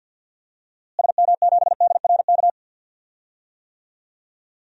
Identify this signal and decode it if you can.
Morse code, heard as a keyed tone, about 50 wpm, 700 Hz. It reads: SM7DRK
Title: DXCC-100 Call Signs - 6 Characters - Repeated 2x - 50wpm